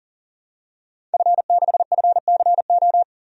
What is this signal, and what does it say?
Morse code, heard as a keyed tone, about 40 wpm, 700 Hz. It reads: F6FCO
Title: DXCC-100 Call Signs - 5 Characters - 40wpm